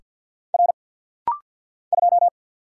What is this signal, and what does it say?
Morse code, heard as a keyed tone, about 50 wpm, 700 Hz. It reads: R 2
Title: Single Letter-Number Alt - Rapid-Fire 50wpm